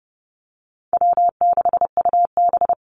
Morse code, heard as a keyed tone, about 30 wpm, 700 Hz. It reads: W6UB
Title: NA SOTA Chaser Callsigns 30wpm